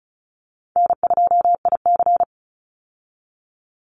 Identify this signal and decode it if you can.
Morse code, heard as a keyed tone, about 34 wpm, 700 Hz. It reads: N2IC
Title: Call Signs Repeated 2x - 35wpm